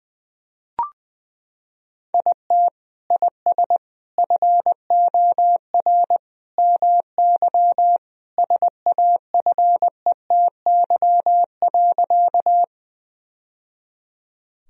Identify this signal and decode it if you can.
Morse code, heard as a keyed tone, about 20 wpm, 700 Hz. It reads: IT IS FOR MY SAFETY.
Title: Sentences from Top 1000 Words - Review Part 4 - 20wpm